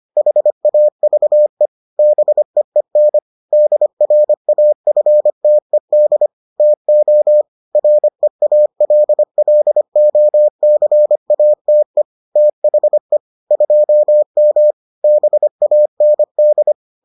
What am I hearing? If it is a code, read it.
Morse code, heard as a keyed tone, about 25 wpm, 600 Hz. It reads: HAVE BEEN DRAFTED TO REALLOCATE THE 2M BAND